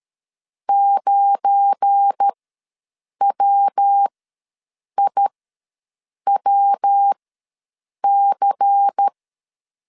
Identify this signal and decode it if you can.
Morse code, heard as a keyed tone, about 13 wpm, spaced as for 7 wpm, 800 Hz. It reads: 9WIWC
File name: LEZIONI / 8 YJ45 / 8 YJ ws 12wpm 3342